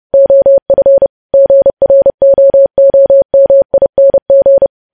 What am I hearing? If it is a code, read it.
Morse code, heard as a keyed tone, about 30 wpm, 550 Hz. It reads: OF GROOMING
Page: news headlines in morse code at 30 wpm